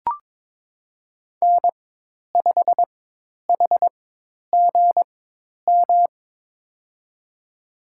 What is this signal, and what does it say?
Morse code, heard as a keyed tone, about 22 wpm, spaced as for 10 wpm, 700 Hz. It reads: N5HGM
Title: USA Call Signs - 5 Characters - 4x Character Spacing - 22wpm